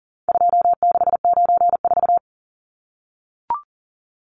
Morse code, heard as a keyed tone, about 40 wpm, 700 Hz. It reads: 2694
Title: Numbers - 4-Digit 40wpm